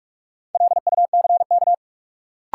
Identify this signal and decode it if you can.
Morse code, heard as a keyed tone, about 46 wpm, 700 Hz. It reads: LUCK
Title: Sentences from Top 1500 Words - New Word Drill Alt - 45wpm